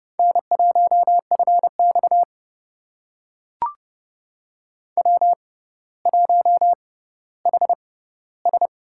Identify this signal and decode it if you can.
Morse code, heard as a keyed tone, about 30 wpm, spaced as for 15 wpm, 700 Hz. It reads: N1FX W1HS
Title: USA Call Signs - 4 Characters - 6x Character Spacing - 30wpm